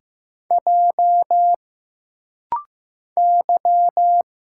Morse code, heard as a keyed tone, about 15 wpm, 700 Hz. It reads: J Y
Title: Single Letters - Rapid-Fire 15wpm